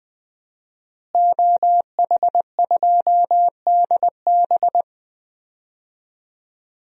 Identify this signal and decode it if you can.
Morse code, heard as a keyed tone, about 20 wpm, 700 Hz. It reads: OH2DB
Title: DXCC-100 Call Signs - 5 Characters - Repeated 2x - 20wpm